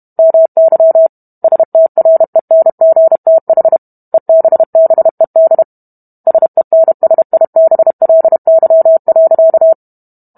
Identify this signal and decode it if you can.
Morse code, heard as a keyed tone, about 32 wpm, 650 Hz. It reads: MY STRENGTH EBBED SENSIBLY.